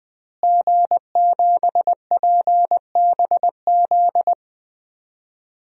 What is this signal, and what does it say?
Morse code, heard as a keyed tone, about 20 wpm, 700 Hz. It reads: G7PBZ
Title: DXCC-100 Call Signs - 5 Characters - Repeated 3x - 20wpm